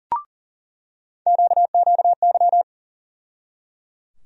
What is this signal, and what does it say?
Morse code, heard as a keyed tone, about 37 wpm, 700 Hz. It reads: QQY
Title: USA Call Sign Postfixes - 3 Letters 40wpm